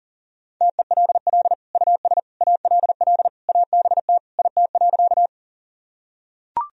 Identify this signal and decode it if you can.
Morse code, heard as a keyed tone, about 40 wpm, 700 Hz. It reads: TELL US ALL ABT IT.
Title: Sentences from Top 200 Words and Abbreviations 40wpm